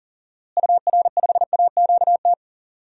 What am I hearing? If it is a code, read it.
Morse code, heard as a keyed tone, about 40 wpm, 700 Hz. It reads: UR5AQT
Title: DXCC-100 Call Signs - 6 Characters - Repeated 2x - 40wpm